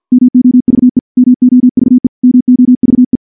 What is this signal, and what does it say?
Morse code, heard as a keyed tone, about 50 wpm, 250 Hz. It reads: MOVE MOVE MOVE
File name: morse / morse-training-50WPM